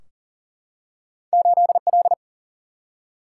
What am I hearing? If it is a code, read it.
Morse code, heard as a keyed tone, about 38 wpm, 700 Hz. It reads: 8L
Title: Single-Letter Single-Number 40wpm